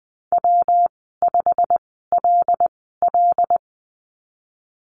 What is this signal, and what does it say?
Morse code, heard as a keyed tone, about 19 wpm, spaced as for 14 wpm, 700 Hz. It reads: W5LL
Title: 161 Lesson 26g - Letter g - Call Signs 20wpm